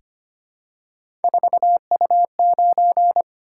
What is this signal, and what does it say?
Morse code, heard as a keyed tone, about 25 wpm, 700 Hz. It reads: 4U9
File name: Three Letter-Number Combinations - 25wpm